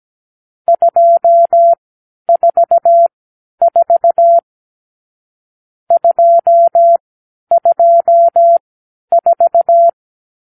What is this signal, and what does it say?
Morse code, heard as a keyed tone, about 17 wpm, spaced as for 10 wpm, 680 Hz. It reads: 244 224
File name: k7qo / Lesson 46 (1-4)